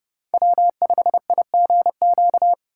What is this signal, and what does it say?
Morse code, heard as a keyed tone, about 31 wpm, 700 Hz. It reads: W5IGQ
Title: USA Call Signs - 5 Characters - 8x Character Spacing - 30wpm